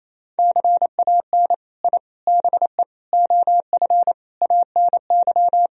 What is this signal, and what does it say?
Morse code, heard as a keyed tone, about 28 wpm, 700 Hz. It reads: CAN I BE OF ANY
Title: Sentences from Top 1000 Words - Review Part 4 - 28wpm